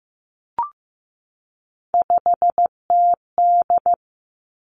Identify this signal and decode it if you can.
Morse code, heard as a keyed tone, about 15 wpm, 700 Hz. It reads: 5TD
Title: Three Challenging Letter-Number Combinations 15wpm